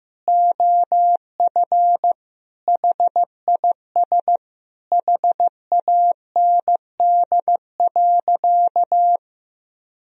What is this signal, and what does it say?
Morse code, heard as a keyed tone, about 15 wpm, 700 Hz. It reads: OF HIS HAND.
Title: Sentences from Top 1000 Words - Review Part 2 - 15wpm